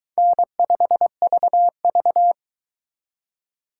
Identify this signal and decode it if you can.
Morse code, heard as a keyed tone, about 22 wpm, 700 Hz. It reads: N5VV